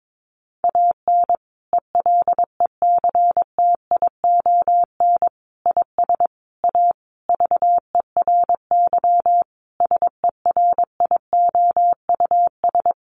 Morse code, heard as a keyed tone, about 22 wpm, 700 Hz. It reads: AN ELECTION IS A VERY SERIOUS
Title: Sentences from Top 1500 Words 22wpm